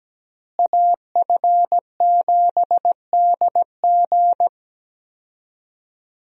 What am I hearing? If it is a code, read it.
Morse code, heard as a keyed tone, about 17 wpm, 700 Hz. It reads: AF7DG
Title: USA Call Signs - 5 Characters - Repeated 2x - 17wpm